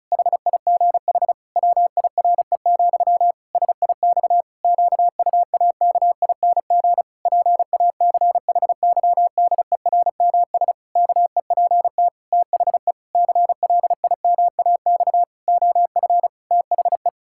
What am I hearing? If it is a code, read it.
Morse code, heard as a keyed tone, about 36 wpm, 700 Hz. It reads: HIGH WIRE, SIX QUAKING PACHYDERMS KEPT THE CLIMAX OF THE
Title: Pangram Challenge v2 - 35wpm